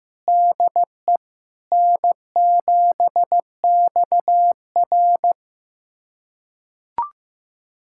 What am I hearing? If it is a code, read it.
Morse code, heard as a keyed tone, about 15 wpm, 700 Hz. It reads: DE N7XR
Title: USA Callsign DE Callsign - 4 Characters - 15wpm